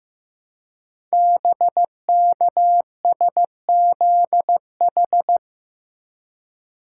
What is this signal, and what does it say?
Morse code, heard as a keyed tone, about 15 wpm, 700 Hz. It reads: BKSZH
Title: Code Groups - 5 Letters - 15wpm